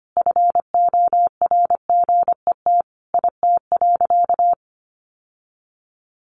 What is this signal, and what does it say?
Morse code, heard as a keyed tone, about 25 wpm, 700 Hz. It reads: FORGET IT.
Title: Sentences from Top 1200 Words 25wpm